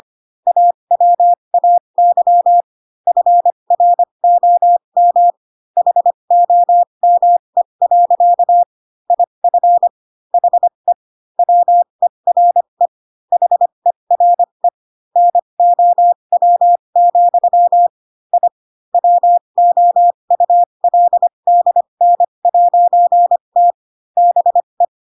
Morse code, heard as a keyed tone, about 25 wpm, spaced as for 22 wpm, 700 Hz. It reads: AWAY FROM HOME. IF HE WERE HERE NOW, I WOULDN'T BE